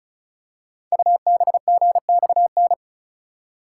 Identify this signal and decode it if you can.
Morse code, heard as a keyed tone, about 35 wpm, 690 Hz. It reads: UBGXN
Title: Code Groups - 5 Letters - 35wpm